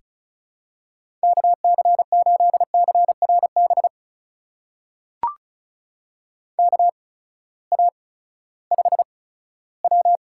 Morse code, heard as a keyed tone, about 36 wpm, spaced as for 18 wpm, 700 Hz. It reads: KC8CRB KA5W
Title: USA Call Signs - 6 Characters - 8x Character Spacing - 35wpm